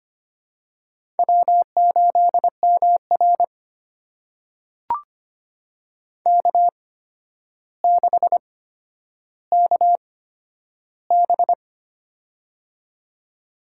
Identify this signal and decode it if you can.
Morse code, heard as a keyed tone, about 25 wpm, spaced as for 11 wpm, 700 Hz. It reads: W8MR K6KB
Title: USA Call Signs - 4 Characters - 8x Character Spacing - 25wpm